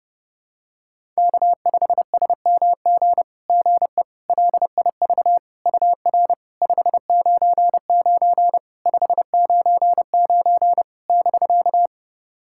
Morse code, heard as a keyed tone, about 30 wpm, 700 Hz. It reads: K5SMG GE LIV UR 599 599 <BK>
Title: SOTA QSO Practice 30wpm